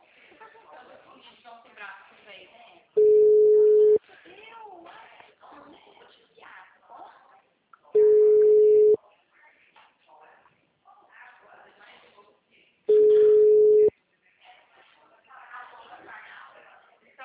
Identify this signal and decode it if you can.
Ringback tone (Europe, 425 Hz), with speech.
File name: out- -2010- - - .44069